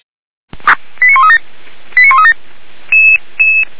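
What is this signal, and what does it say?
Five-tone selective call (ZVEI), heard as quick runs of five tones, tones of about 80 ms. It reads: 89137 89137
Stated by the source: ZVEI five-tone selective call 89135 received via SDR, from feuerwehr-orlamuende (Feuerwehr Orlamünde fire brigade).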